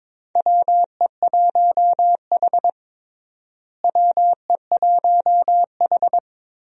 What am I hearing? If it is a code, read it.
Morse code, heard as a keyed tone, about 22 wpm, 700 Hz. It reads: WE1H WE1H
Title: USA Call Signs - 4 Characters - Repeated 3x - 22wpm